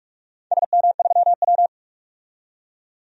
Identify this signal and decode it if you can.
Morse code, heard as a keyed tone, about 44 wpm, 700 Hz. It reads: SM3W
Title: DXCC-100 Call Signs - 4 Characters - 45wpm